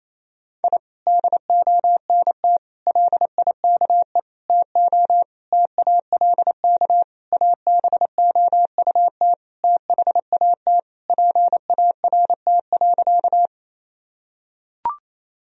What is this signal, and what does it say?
Morse code, heard as a keyed tone, about 27 wpm, 700 Hz. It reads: I DONT LIKE TO TALK ABOUT THAT PART.
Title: Sentences from Top 1000 Words - Review Part 1 - 28wpm